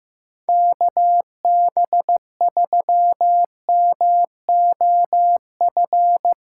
Morse code, heard as a keyed tone, about 15 wpm, 700 Hz. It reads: KB3MOF
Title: USA Callsign DE Callsign 15wpm